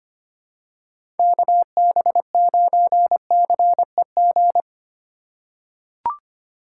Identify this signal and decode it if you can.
Morse code, heard as a keyed tone, about 25 wpm, 700 Hz. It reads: KB9CEG
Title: USA Call Signs - 4x Character Spacing - 25wpm